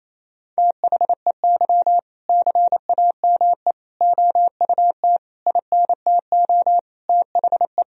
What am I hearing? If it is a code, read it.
Morse code, heard as a keyed tone, about 27 wpm, 700 Hz. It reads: THEY CAME OUT INTO THE
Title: Sentences from Top 1000 Words - Review Part 3 - 28wpm